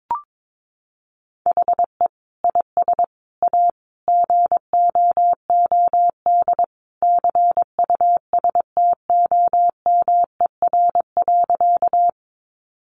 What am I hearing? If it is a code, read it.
Morse code, heard as a keyed tone, about 22 wpm, 700 Hz. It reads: HE IS A GOOD CUSTOMER.
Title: Sentences from Top 1100 Words 22wpm